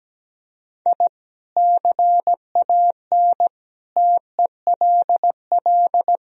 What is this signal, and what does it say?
Morse code, heard as a keyed tone, about 17 wpm, 700 Hz. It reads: I CAN TELL
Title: Sentences from Top 1000 Words - Review Part 1 - 17wpm